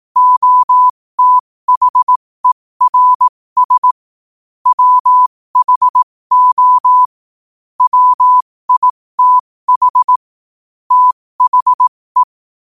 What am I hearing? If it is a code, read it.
Morse code, heard as a keyed tone, about 18 wpm, spaced as for 15 wpm, 1000 Hz. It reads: OTHERS WHO WITH THE
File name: qotdfiles / 15wpm